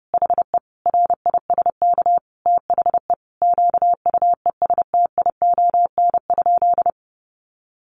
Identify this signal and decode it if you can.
Morse code, heard as a keyed tone, about 30 wpm, 700 Hz. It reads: HE RISK THE QUESTION?
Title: Sentences from Top 600 Words 30wpm